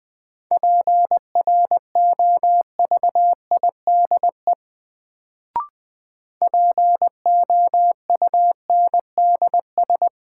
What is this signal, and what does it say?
Morse code, heard as a keyed tone, about 21 wpm, 700 Hz. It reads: PROVIDE POUNDS
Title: Sentences from Top 1000 Words - Review Part 2 - New Word Drill - 20wpm